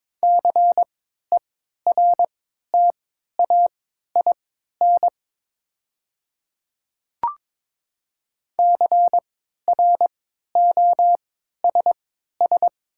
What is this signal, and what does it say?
Morse code, heard as a keyed tone, about 22 wpm, spaced as for 12 wpm, 700 Hz. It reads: CERTAIN CROSS
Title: 3x Character Spacing - Top 500 Words - 22wpm